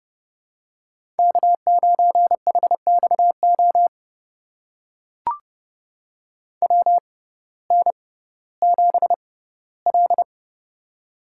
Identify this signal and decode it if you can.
Morse code, heard as a keyed tone, about 30 wpm, spaced as for 17 wpm, 700 Hz. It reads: K9HXO WN7L
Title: USA Call Signs - 6x Character Spacing - 30wpm